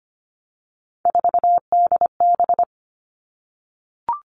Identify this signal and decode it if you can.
Morse code, heard as a keyed tone, about 25 wpm, 700 Hz. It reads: 4DB
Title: All Cut Numbers - 3-Digit 25wpm